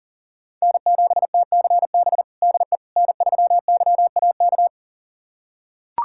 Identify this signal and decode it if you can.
Morse code, heard as a keyed tone, about 37 wpm, 700 Hz. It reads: N7TCB DE N3YAK
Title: USA Callsign DE Callsign - 5 Characters - 40wpm